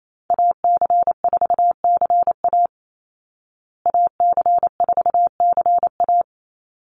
Morse code, heard as a keyed tone, about 28 wpm, 700 Hz. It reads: AC4CA AC4CA
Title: Call Signs Repeated 3x - 28wpm